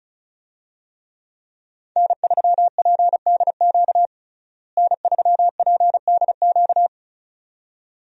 Morse code, heard as a keyed tone, about 34 wpm, 700 Hz. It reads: N3PDQ N3PDQ
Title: USA Call Signs - 5 Characters - Repeated 2x - 35wpm